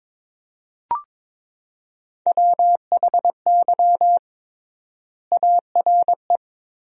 Morse code, heard as a keyed tone, about 22 wpm, 700 Hz. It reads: WHY ARE
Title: 3x Word-Spacing - Sentences from Top 200 Words - 22wpm